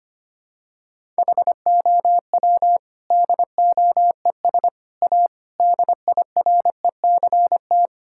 Morse code, heard as a keyed tone, about 25 wpm, 700 Hz. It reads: HOW DOES A DIRECT
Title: Sentences from Top 1000 Words - Review Part 5 - 25wpm